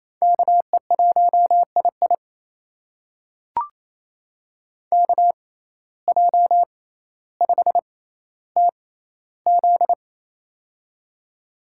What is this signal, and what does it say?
Morse code, heard as a keyed tone, about 28 wpm, spaced as for 14 wpm, 700 Hz. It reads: KE1II KJ5TZ
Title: USA Call Signs - 5 Characters - 6x Character Spacing - 28wpm